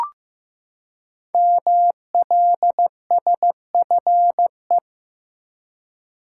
Morse code, heard as a keyed tone, about 15 wpm, 700 Hz. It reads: MLSFE
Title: Code Groups - 5 Letters - 15wpm